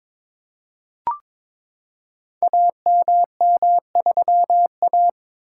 Morse code, heard as a keyed tone, about 22 wpm, 700 Hz. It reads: AMM3A
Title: Code Groups - 5 Letter:Numbers - 22wpm